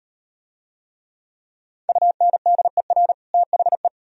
Morse code, heard as a keyed tone, about 38 wpm, 700 Hz. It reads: UNDER THE